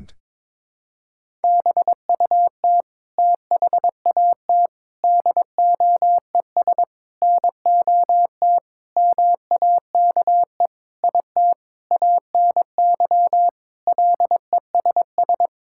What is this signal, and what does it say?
Morse code, heard as a keyed tone, about 22 wpm, 700 Hz. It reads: BUT THAT DOES NOT MAKE IT ANY LESS
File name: Sentences with 7-Word Phrases - 22wpm